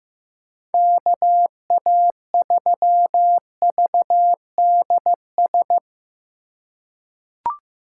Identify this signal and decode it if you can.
Morse code, heard as a keyed tone, about 15 wpm, 700 Hz. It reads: KA3VDS
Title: USA Call Signs - 6 Characters - 4x Character Spacing - 15wpm